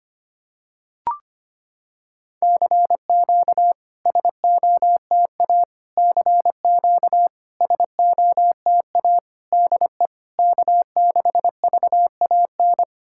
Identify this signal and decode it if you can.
Morse code, heard as a keyed tone, about 25 wpm, 700 Hz. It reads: CQ SOTA CQ SOTA DE K6VAN
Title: SOTA QSO Practice 25wpm